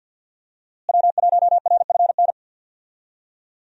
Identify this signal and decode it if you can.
Morse code, heard as a keyed tone, about 47 wpm, 700 Hz. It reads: W1RFN
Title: USA Call Signs - 5 Characters - Repeated 3x - 50wpm